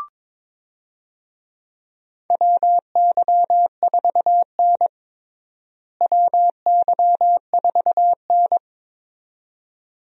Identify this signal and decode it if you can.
Morse code, heard as a keyed tone, about 22 wpm, 700 Hz. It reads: WY4N WY4N
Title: USA Call Signs - 4 Characters - Repeated 2x - 22wpm